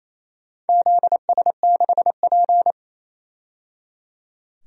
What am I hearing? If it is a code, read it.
Morse code, heard as a keyed tone, about 28 wpm, 700 Hz. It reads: ZS6P
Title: DXCC-100 Call Signs - 4 Characters - 28wpm